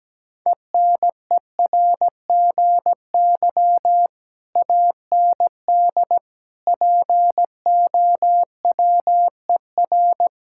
Morse code, heard as a keyed tone, about 17 wpm, 700 Hz. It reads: ENERGY AND POWER
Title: Sentences from Top 1000 Words - Review Part 6 - 17wpm